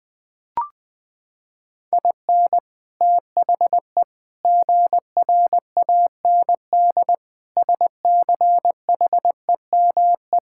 Morse code, heard as a keyed tone, about 20 wpm, 700 Hz. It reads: IN THE GRAND SCHEME
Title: Sets of 6 Words - 20wpm